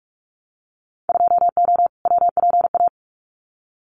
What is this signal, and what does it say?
Morse code, heard as a keyed tone, about 44 wpm, 700 Hz. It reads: 2O WPA
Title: Winter Field Day Exchanges 45wpm